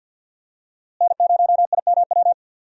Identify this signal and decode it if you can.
Morse code, heard as a keyed tone, about 52 wpm, 700 Hz. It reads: N0IMW
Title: USA Call Signs - 5 Characters - Repeated 2x - 50wpm